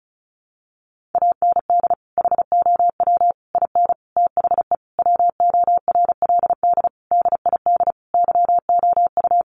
Morse code, heard as a keyed tone, about 34 wpm, 700 Hz. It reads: AND HOW IN THE WORLD DID YOU
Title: Sentences from Top 200 Words 35wpm